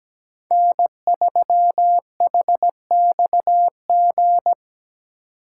Morse code, heard as a keyed tone, about 17 wpm, 700 Hz. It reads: N3HXG
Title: USA Call Signs - 5 Characters - 8x Character Spacing - 17wpm